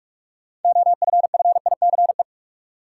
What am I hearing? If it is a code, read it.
Morse code, heard as a keyed tone, about 46 wpm, 700 Hz. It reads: OFFICE
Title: Sentences from Top 1000 Words - New Word Drill - 45wpm